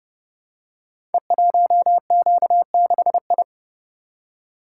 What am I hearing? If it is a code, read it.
Morse code, heard as a keyed tone, about 30 wpm, 700 Hz. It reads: E1Q6I
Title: Code Groups - 5 Letter:Numbers - 30wpm